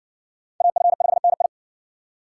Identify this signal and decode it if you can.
Morse code, heard as a keyed tone, about 57 wpm, 700 Hz. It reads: AF5NI